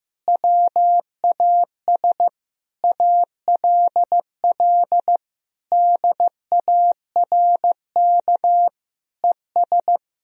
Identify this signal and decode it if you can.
Morse code, heard as a keyed tone, about 16 wpm, 700 Hz. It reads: WAS ALL DARK ES
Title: Sentences from Top 500 Words and Abbreviations 15wpm